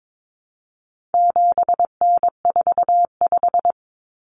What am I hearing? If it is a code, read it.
Morse code, heard as a keyed tone, about 22 wpm, 700 Hz. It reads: 7N45
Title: Common Cut Numbers - 4-Digit 22wpm